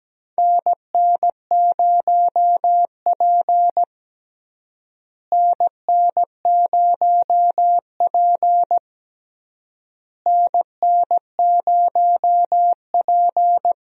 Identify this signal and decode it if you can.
Morse code, heard as a keyed tone, about 17 wpm, 700 Hz. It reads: NN0P NN0P NN0P
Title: USA Call Signs - 4 Characters - Repeated 3x - 17wpm